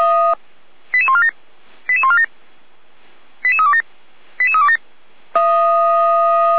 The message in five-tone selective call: 88147 88147 88327 88327

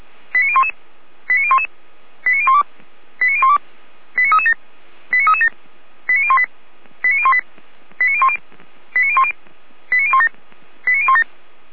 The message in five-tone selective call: 89011 89011 89012 89012 89337 89337 89018 89018 89010 89010 89017 89017

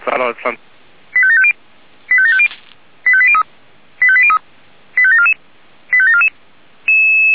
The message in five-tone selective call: 85750 85750 85803 85803 85744 85744

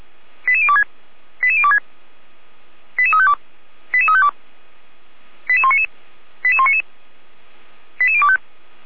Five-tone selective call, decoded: 88026 88026 88352 88352 88199 88199 88025

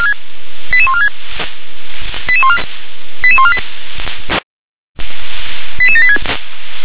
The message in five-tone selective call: 88146 88148 88786

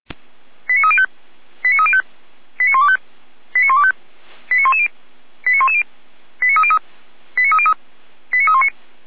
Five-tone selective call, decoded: 89305 89305 89125 89125 89110 89110 89303 89319

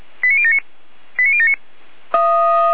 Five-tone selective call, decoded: 89079 89079

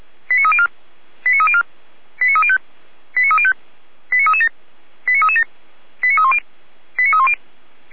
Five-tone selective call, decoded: 89304 89304 89305 89305 89337 89337 89310 89310

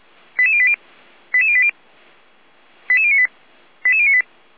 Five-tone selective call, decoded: 88080 88080 88098 88098

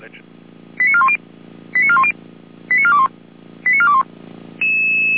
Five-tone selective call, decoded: 89410 89410 89421 89421

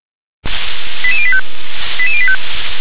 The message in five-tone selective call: 88085 88085

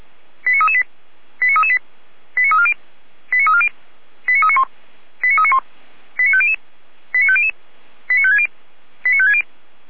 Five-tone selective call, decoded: 89338 89338 89340 89340 89391 89391 89500 89500 89560 89560

